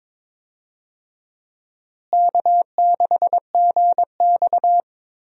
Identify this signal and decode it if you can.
Morse code, heard as a keyed tone, about 22 wpm, 700 Hz. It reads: K6GX
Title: USA Call Signs - 4 Characters - Repeated 2x - 22wpm